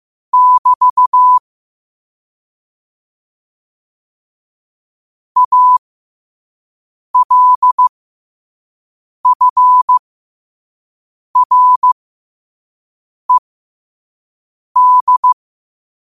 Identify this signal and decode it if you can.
Morse code, heard as a keyed tone, about 15 wpm, spaced as for 5 wpm, 1000 Hz. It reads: <BT> ALFRED
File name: qotdfiles / 5wpm